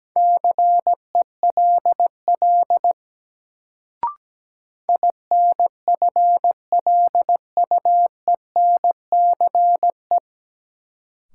Morse code, heard as a keyed tone, about 17 wpm, 700 Hz. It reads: CELL INFLUENCE
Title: Sentences from Top 1100 Words - New Word Drill Alt - 17wpm